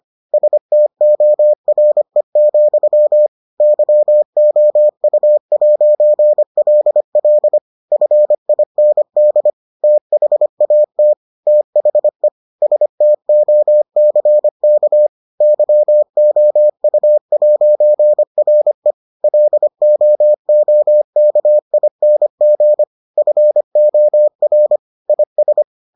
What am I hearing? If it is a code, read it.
Morse code, heard as a keyed tone, about 25 wpm, 600 Hz. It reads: STORE, YOU'LL FIND THAT THE STOCK YOU'RE LOOKING FOR IS